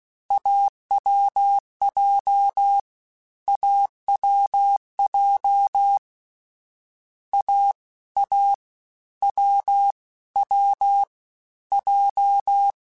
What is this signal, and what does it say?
Morse code, heard as a keyed tone, about 16 wpm, 800 Hz. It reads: AWJ AWJ A A W W J